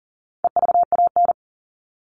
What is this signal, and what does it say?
Morse code, heard as a keyed tone, about 40 wpm, 700 Hz. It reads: EVAN
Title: All Cut Numbers - 4-Digit 40wpm